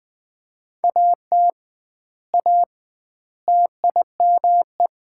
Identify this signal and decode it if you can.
Morse code, heard as a keyed tone, about 19 wpm, 700 Hz. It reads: AT A TIME
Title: 2x Word-Spacing - Sets of 4 Words - 20wpm